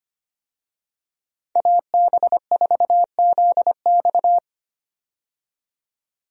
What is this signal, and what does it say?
Morse code, heard as a keyed tone, about 25 wpm, 700 Hz. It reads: AB4ZX
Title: USA Call Signs - 5 Characters - 25wpm